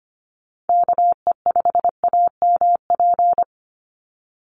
Morse code, heard as a keyed tone, about 25 wpm, 700 Hz. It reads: KE5AMP
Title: USA Call Signs 25wpm